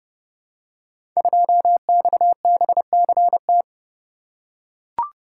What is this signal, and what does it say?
Morse code, heard as a keyed tone, about 30 wpm, 700 Hz. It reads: 2XBCT
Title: Code Groups - 5 Letter:Numbers - 30wpm